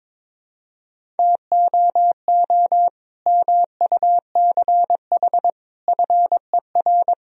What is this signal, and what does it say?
Morse code, heard as a keyed tone, about 22 wpm, 700 Hz. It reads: TOO MUCH FER
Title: Sentences from Top 200 Words and Abbreviations 22wpm